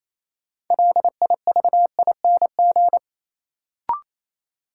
Morse code, heard as a keyed tone, about 28 wpm, 700 Hz. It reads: LIVING
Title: Sentences from Top 1000 Words - Review Part 6 - New Word Drill - 28wpm